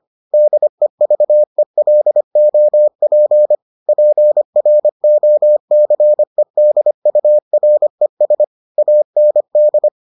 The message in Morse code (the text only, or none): DEVELOP PROCEDURES AND